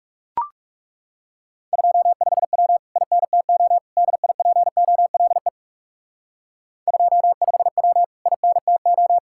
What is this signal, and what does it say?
Morse code, heard as a keyed tone, about 37 wpm, 700 Hz. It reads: 25W INTO DIPOLE 25W INTO
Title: Speed-Racing Common QSO Phrases Encore - 30wpm